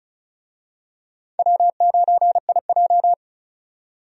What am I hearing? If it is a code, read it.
Morse code, heard as a keyed tone, about 36 wpm, 700 Hz. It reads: W9IJ
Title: USA Call Signs - 4 Characters - 35wpm